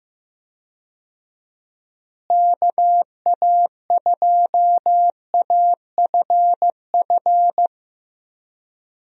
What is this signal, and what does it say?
Morse code, heard as a keyed tone, about 15 wpm, 700 Hz. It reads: KA2AFF
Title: USA Call Signs - 6 Characters - Repeated 3x - 15wpm